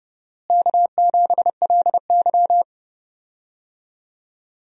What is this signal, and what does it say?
Morse code, heard as a keyed tone, about 30 wpm, 700 Hz. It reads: K7LY
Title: USA Call Signs - 4 Characters - Repeated 3x - 30wpm